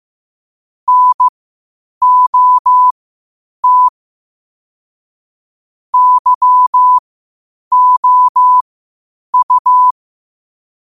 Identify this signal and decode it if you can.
Morse code, heard as a keyed tone, about 15 wpm, spaced as for 8 wpm, 1000 Hz. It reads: NOT YOU
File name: qotdfiles / 8wpm